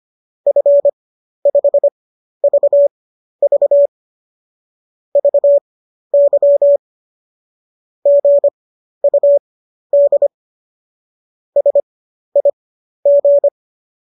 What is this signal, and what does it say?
Morse code, heard as a keyed tone, about 25 wpm, spaced as for 12 wpm, 570 Hz. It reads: F5VV VY GUD SIG
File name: qso009 12